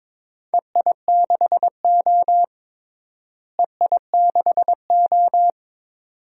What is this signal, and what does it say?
Morse code, heard as a keyed tone, about 22 wpm, 700 Hz. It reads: EI6O EI6O
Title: DXCC-100 Call Signs - 4 Characters - Repeated 2x - 22wpm